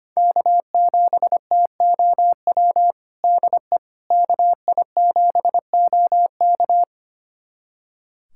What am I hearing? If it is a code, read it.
Morse code, heard as a keyed tone, about 25 wpm, 700 Hz. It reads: K7TOW DE KI7OK
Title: USA Callsign DE Callsign - 5 Characters - 25wpm